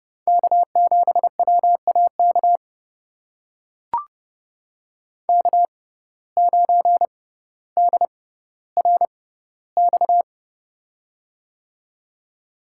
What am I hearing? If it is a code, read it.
Morse code, heard as a keyed tone, about 31 wpm, spaced as for 16 wpm, 700 Hz. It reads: K7WAK K9DRX
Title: USA Call Signs - 5 Characters - 6x Character Spacing - 30wpm